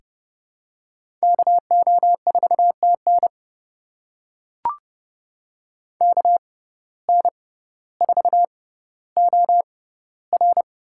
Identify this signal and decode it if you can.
Morse code, heard as a keyed tone, about 32 wpm, spaced as for 16 wpm, 700 Hz. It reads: KO4TN KN4OR
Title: USA Call Signs - 6x Character Spacing - 30wpm